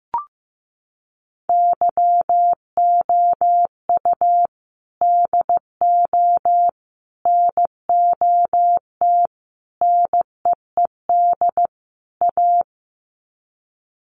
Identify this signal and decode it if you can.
Morse code, heard as a keyed tone, about 15 wpm, 700 Hz. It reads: YOU DO NOT NEED A
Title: Sets of 5 Words Encore 15wpm